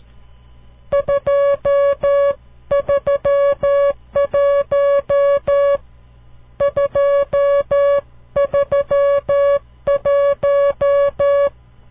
Morse code, heard as a keyed tone, about 13 wpm, 560 Hz. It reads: 231 231
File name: CW Learning Programs / farnsworth / unit 5